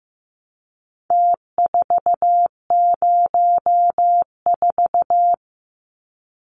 Morse code, heard as a keyed tone, about 15 wpm, 700 Hz. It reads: T404
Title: Numbers - Contest Style - 4-Digit 15wpm